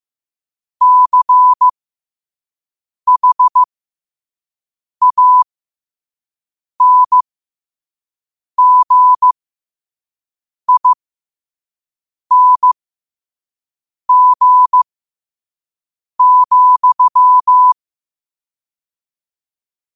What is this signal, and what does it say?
Morse code, heard as a keyed tone, about 15 wpm, spaced as for 5 wpm, 1000 Hz. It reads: CHANGING,
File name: 5wpm